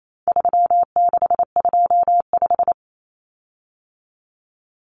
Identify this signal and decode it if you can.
Morse code, heard as a keyed tone, about 28 wpm, 700 Hz. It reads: 3625
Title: Common Cut Numbers - 4-Digit 28wpm